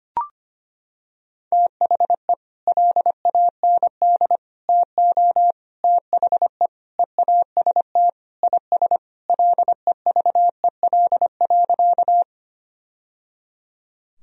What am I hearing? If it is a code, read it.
Morse code, heard as a keyed tone, about 25 wpm, 700 Hz. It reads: THE LAND TO THE EAST IS LEVEL.
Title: Sentences from Top 1000 Words - Review Part 4 - 25wpm